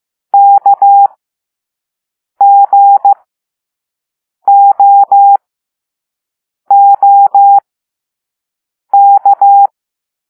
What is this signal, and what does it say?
Morse code, heard as a keyed tone, about 15 wpm, spaced as for 5 wpm, 800 Hz. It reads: KGOOK
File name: Morse / Level4A